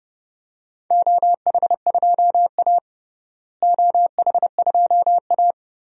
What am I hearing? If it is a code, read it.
Morse code, heard as a keyed tone, about 30 wpm, 700 Hz. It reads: OH2A OH2A
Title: DXCC-100 Call Signs - 4 Characters - Repeated 3x - 30wpm